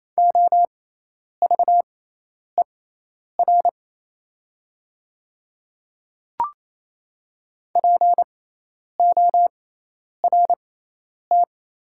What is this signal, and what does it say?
Morse code, heard as a keyed tone, about 27 wpm, spaced as for 10 wpm, 700 Hz. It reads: OVER PORT
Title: 6x Character Spacing - Top 300 Words - 28wpm